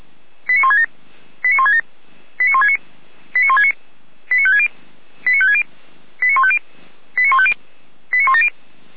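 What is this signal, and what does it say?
Five-tone selective call (ZVEI), heard as quick runs of five tones, tones of about 70 ms. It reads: 89167 89167 89169 89169 89560 89560 89150 89150 89170
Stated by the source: ZVEI five-tone call received by SDR, from the Orlamünde fire brigade (feuerwehr-orlamuende) server.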